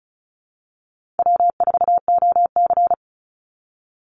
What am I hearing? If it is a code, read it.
Morse code, heard as a keyed tone, about 34 wpm, 700 Hz. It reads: W4OC
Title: Easy Call Signs Repeated 3x - 35wpm